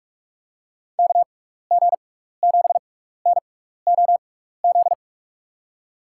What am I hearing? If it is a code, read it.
Morse code, heard as a keyed tone, about 44 wpm, spaced as for 15 wpm, 700 Hz. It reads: KG7NOZ
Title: USA Call Signs - 6 Characters - 6x Character Spacing - 45wpm